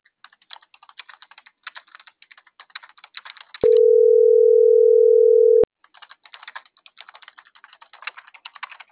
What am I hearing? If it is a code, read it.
Ringback tone (North America), 440 + 480 Hz.
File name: CENTRAL MAERSK / out luis.herrera - .82072